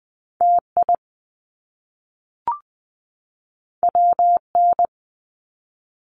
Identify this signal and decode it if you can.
Morse code, heard as a keyed tone, about 20 wpm, 700 Hz. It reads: TI WN
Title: Top Two Letter Combinations in Words 20wpm